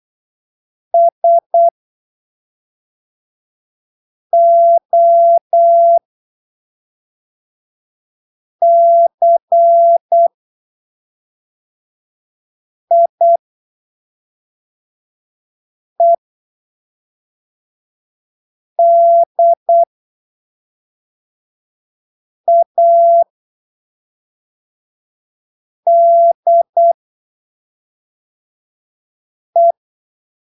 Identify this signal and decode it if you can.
Morse code, heard as a keyed tone, about 8 wpm, spaced as for 3 wpm, 680 Hz. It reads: SOCIEDADE
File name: arquivos / texto / texto 30 08ppm